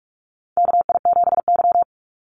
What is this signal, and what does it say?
Morse code, heard as a keyed tone, about 44 wpm, 700 Hz. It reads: KI7Y
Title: Call Signs Repeated 2x - 45wpm